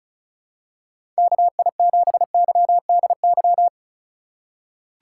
Morse code, heard as a keyed tone, about 34 wpm, 700 Hz. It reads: KI7YDY
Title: USA Call Signs - 6 Characters - 4x Character Spacing - 35wpm